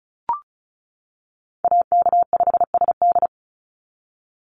Morse code, heard as a keyed tone, about 34 wpm, 700 Hz. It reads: AK5SD
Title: NA SOTA Chaser Callsigns 35wpm